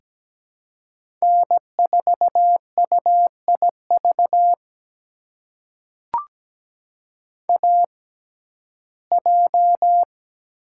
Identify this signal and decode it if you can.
Morse code, heard as a keyed tone, about 17 wpm, spaced as for 12 wpm, 700 Hz. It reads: N4UIV AJ
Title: USA Call Signs - 6x Character Spacing - 17wpm